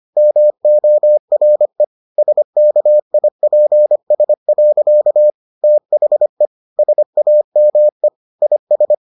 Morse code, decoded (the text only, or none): MORE SKIPS. THE SAME IS